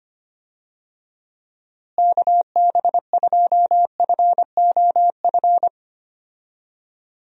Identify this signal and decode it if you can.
Morse code, heard as a keyed tone, about 25 wpm, 700 Hz. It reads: KB2FOF
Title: USA Call Signs - 6 Characters - Repeated 2x - 25wpm